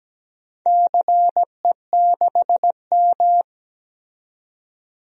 Morse code, heard as a keyed tone, about 17 wpm, 700 Hz. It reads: CE6M
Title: DXCC-100 Call Signs - 4 Characters - 17wpm